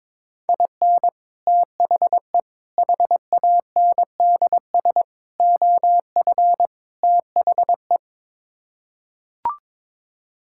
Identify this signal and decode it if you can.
Morse code, heard as a keyed tone, about 22 wpm, 700 Hz. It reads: IN THE HANDS OF THE
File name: practice / 22wpm / Build A Phrase - 5-Word Phrases - 22wpm